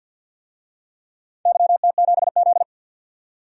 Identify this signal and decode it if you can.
Morse code, heard as a keyed tone, about 50 wpm, 700 Hz. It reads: YT7Z
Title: DXCC-100 Call Signs - 4 Characters - 50wpm